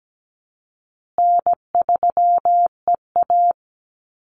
Morse code, heard as a keyed tone, about 17 wpm, 700 Hz. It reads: N3EA
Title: Easy Call Signs Repeated 3x - 17wpm